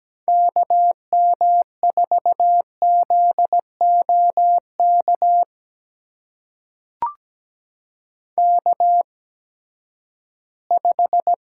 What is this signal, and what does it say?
Morse code, heard as a keyed tone, about 17 wpm, spaced as for 12 wpm, 700 Hz. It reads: KM4ZOK K5
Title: USA Call Signs - 8x Character Spacing - 17wpm